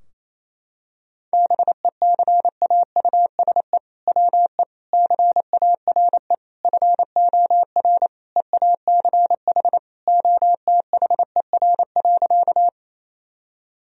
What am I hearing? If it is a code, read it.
Morse code, heard as a keyed tone, about 28 wpm, 700 Hz. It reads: BECAUSE WE CARE FOR EACH OTHER.
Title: Sentences from Top 1000 Words - Review Part 3 - 28wpm